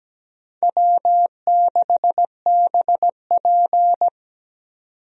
Morse code, heard as a keyed tone, about 17 wpm, 700 Hz. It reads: W6BP
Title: USA Call Signs - 4 Characters - 17wpm